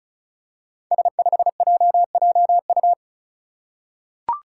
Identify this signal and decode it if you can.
Morse code, heard as a keyed tone, about 36 wpm, 700 Hz. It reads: S5JJU
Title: Five Challenging Letter-Number Combinations 35wpm